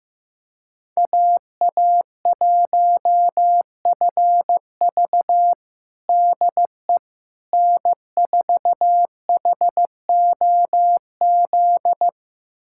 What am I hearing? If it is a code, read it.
Morse code, heard as a keyed tone, about 15 wpm, 700 Hz. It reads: AA1FV DE N4HOZ
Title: USA Callsign DE Callsign - 5 Characters - 15wpm